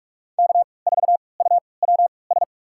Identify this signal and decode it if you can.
Morse code, heard as a keyed tone, about 41 wpm, spaced as for 25 wpm, 700 Hz. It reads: K4UWS
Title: USA Call Signs - 5 Characters - 3x Character Spacing - 45wpm